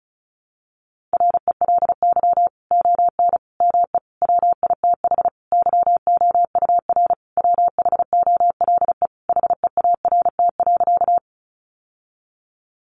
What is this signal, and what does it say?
Morse code, heard as a keyed tone, about 34 wpm, 700 Hz. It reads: RELY ON ME WITH YOUR WHOLE HEART.
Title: Sentences from Top 1200 Words 35wpm